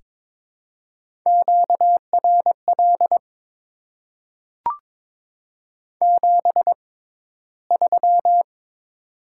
Morse code, heard as a keyed tone, about 23 wpm, spaced as for 14 wpm, 700 Hz. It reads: QRL 73
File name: Top 20 QSO Elements - 6x Character Spacing - 22wpm